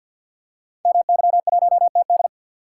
Morse code, heard as a keyed tone, about 50 wpm, 700 Hz. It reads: MY1TD
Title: Code Groups - 5 Letter:Numbers - 50wpm